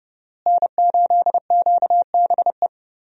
Morse code, heard as a keyed tone, about 32 wpm, 700 Hz. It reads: N8QBE
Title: USA Call Signs - 5 Characters - Repeated 2x - 30wpm